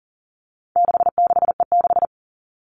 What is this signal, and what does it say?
Morse code, heard as a keyed tone, about 37 wpm, 700 Hz. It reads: 66E6
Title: All Cut Numbers - 4-Digit 40wpm